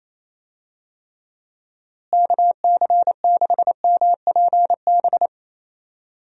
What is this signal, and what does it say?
Morse code, heard as a keyed tone, about 28 wpm, 700 Hz. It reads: KC6MPB
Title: USA Call Signs - 6 Characters - Repeated 3x - 28wpm